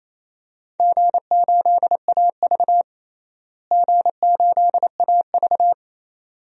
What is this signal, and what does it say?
Morse code, heard as a keyed tone, about 28 wpm, 700 Hz. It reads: G8AV G8AV
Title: DXCC-100 Call Signs - 4 Characters - Repeated 2x - 28wpm